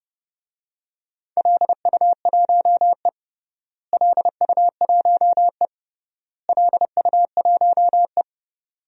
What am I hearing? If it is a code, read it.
Morse code, heard as a keyed tone, about 30 wpm, 700 Hz. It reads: LU1E LU1E LU1E
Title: DXCC-100 Call Signs - 4 Characters - Repeated 3x - 30wpm